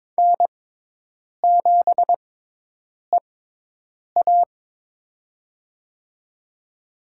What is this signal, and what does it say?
Morse code, heard as a keyed tone, about 22 wpm, spaced as for 8 wpm, 700 Hz. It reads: N7EA
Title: USA Call Signs - 4 Characters - 6x Character Spacing - 22wpm